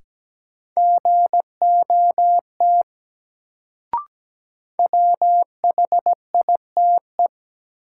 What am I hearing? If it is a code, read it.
Morse code, heard as a keyed tone, about 17 wpm, 700 Hz. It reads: GOT WHITE
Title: Sets of 2 Words - New Word Drill - 17wpm